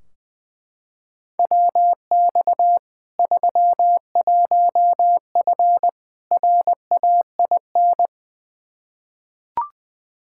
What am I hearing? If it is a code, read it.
Morse code, heard as a keyed tone, about 20 wpm, 700 Hz. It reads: WX 31F RAIN
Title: Practice QSO - Exchange 2 - Info - 20wpm